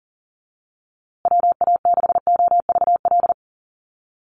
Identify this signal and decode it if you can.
Morse code, heard as a keyed tone, about 40 wpm, 700 Hz. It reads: WA6OVL
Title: USA Call Signs Repeated 2x 40wpm